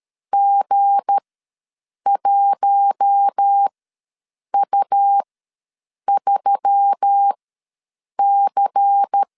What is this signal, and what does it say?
Morse code, heard as a keyed tone, about 13 wpm, spaced as for 7 wpm, 800 Hz. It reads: G1U3C
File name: LEZIONI / 7 GF9Z / 7 G ws 12wpm 3140